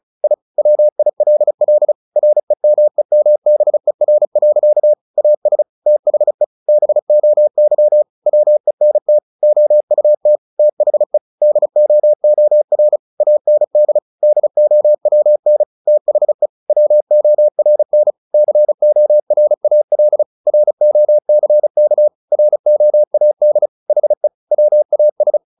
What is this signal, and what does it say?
Morse code, heard as a keyed tone, about 34 wpm, 600 Hz. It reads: I WILL REMEMBER. AS THE BOY WENT OUT THE DOOR AND DOWN THE WORN CORAL ROCK ROAD HE WAS
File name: the old man and the sea 2 35wpm